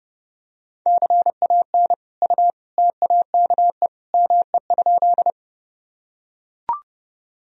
Morse code, heard as a keyed tone, about 30 wpm, 700 Hz. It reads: CAN U TAKE ME?
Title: Sentences from Top 200 Words and Abbreviations 30wpm